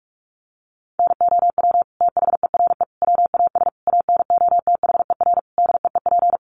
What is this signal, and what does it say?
Morse code, heard as a keyed tone, about 44 wpm, 700 Hz. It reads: NOW THERE WAS ANOTHER DEEP
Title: Sentences from Top 1300 Words 45wpm